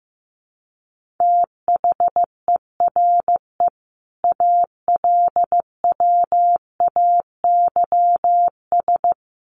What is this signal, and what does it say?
Morse code, heard as a keyed tone, about 15 wpm, 700 Hz. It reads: THERE ALWAYS
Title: Sentences from Top 1100 Words 15wpm